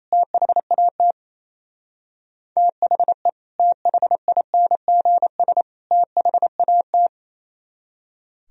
Morse code, heard as a keyed tone, about 27 wpm, 700 Hz. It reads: THAT THE THINGS THAT
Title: Speed-Racing - Sets of 3 Words 28wpm